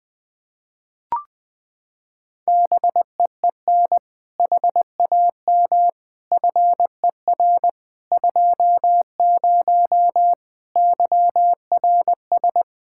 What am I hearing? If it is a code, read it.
Morse code, heard as a keyed tone, about 20 wpm, 700 Hz. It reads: BEEN HAM FER 20 YRS
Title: Practice QSO - Exchange 3 - More Info - 20wpm